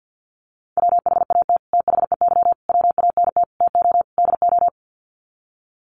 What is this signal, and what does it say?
Morse code, heard as a keyed tone, about 50 wpm, 700 Hz. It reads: WHAT THEY WANT TO DO
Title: Sets of 5 Words Encore2 50wpm